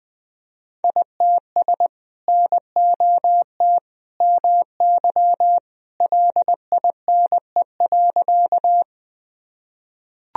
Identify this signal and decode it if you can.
Morse code, heard as a keyed tone, about 21 wpm, 700 Hz. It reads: ITS NOT MY LINE.
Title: Sentences from Top 1000 Words - Review Part 1 - 20wpm